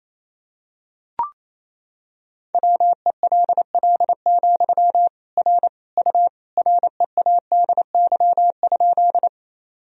Morse code, heard as a keyed tone, about 28 wpm, 700 Hz. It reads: WELL, R U READY?
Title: Sentences from Top 400 Words and Abbreviations 28wpm